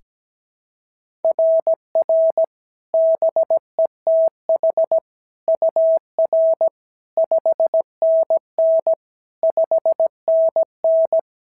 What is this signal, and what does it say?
Morse code, heard as a keyed tone, about 16 wpm, 650 Hz. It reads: RR BETH UR 5NN 5NN
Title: SOTA QSO Practice 17wpm